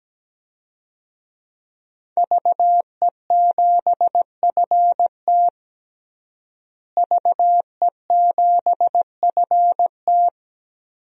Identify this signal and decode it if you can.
Morse code, heard as a keyed tone, about 17 wpm, 700 Hz. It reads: VE7FT VE7FT
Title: DXCC-100 Call Signs - 5 Characters - Repeated 2x - 17wpm